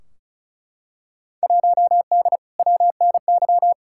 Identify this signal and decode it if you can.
Morse code, heard as a keyed tone, about 36 wpm, 700 Hz. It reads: 1D WNY
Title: Field Day Exchanges - 35wpm